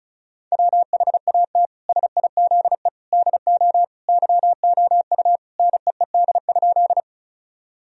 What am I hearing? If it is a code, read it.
Morse code, heard as a keyed tone, about 36 wpm, 700 Hz. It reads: WHAT SIZE DO YOU NEED?
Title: Sentences from Top 1000 Words - Review Part 2 - 35wpm